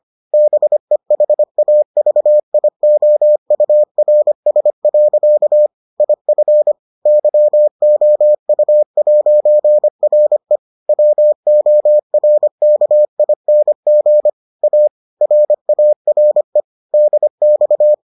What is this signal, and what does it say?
Morse code, heard as a keyed tone, about 25 wpm, 600 Hz. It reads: BEHAVIOURS. IF YOU'RE WORKING A RARE DX